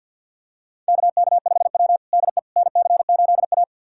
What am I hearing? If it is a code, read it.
Morse code, heard as a keyed tone, about 53 wpm, 690 Hz. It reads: KK5W DE NC8A
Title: USA Callsign DE Callsign - 4 Characters - 50wpm